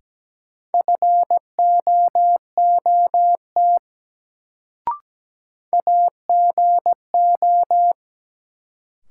Morse code, heard as a keyed tone, about 17 wpm, 700 Hz. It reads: FOOT AGO
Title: Sentences from Top 500 Words - New Word Drill - 17wpm